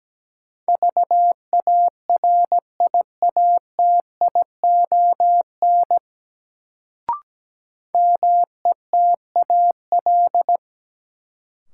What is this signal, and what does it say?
Morse code, heard as a keyed tone, about 17 wpm, 700 Hz. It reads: VARIATION METAL
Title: Sentences from Top 1600 Words - New Word Drill Alt - 17wpm